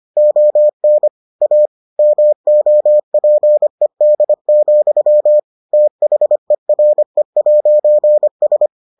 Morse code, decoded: ON A MOPED, THERE'S

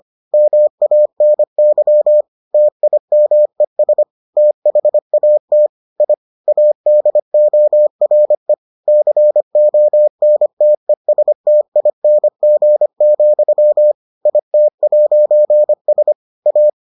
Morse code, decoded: MANY TIMES THAT I ADORE CONTESTING, IT'S A

